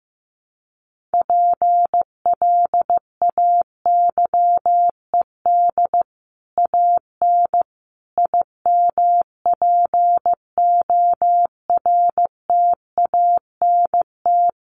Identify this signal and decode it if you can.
Morse code, heard as a keyed tone, about 15 wpm, 700 Hz. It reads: PLAYED AN IMPORTANT
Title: Sets of 5 Words Encore2 15wpm